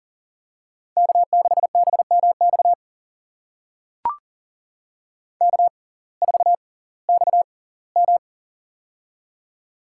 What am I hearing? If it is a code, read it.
Morse code, heard as a keyed tone, about 40 wpm, spaced as for 22 wpm, 700 Hz. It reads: K6BMX K4XM
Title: USA Call Signs - 6x Character Spacing - 40wpm